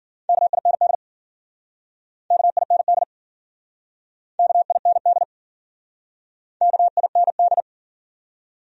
Morse code, heard as a keyed tone, about 52 wpm, 700 Hz. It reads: KIND KIND KIND KIND